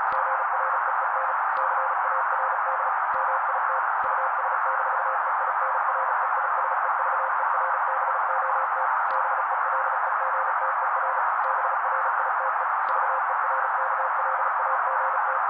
Morse code, heard as a keyed tone, about 36 wpm, 530 Hz. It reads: GREEN CARD MIT US-EINREISEVERBOT BELEGTER BRITE VERKLAGT
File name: daily / 35wpm noise